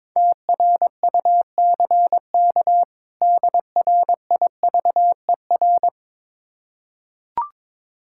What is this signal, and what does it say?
Morse code, heard as a keyed tone, about 22 wpm, 700 Hz. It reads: TRUCK DRIVER
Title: Practice QSO - Exchange 3 - More Info - 22wpm